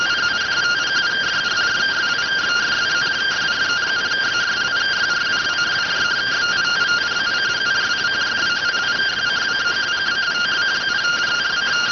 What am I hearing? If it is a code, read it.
Two-tone teleprinter signal, about 170 Hz shift.